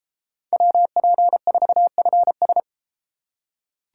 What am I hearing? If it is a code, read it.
Morse code, heard as a keyed tone, about 32 wpm, 700 Hz. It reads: WP4FS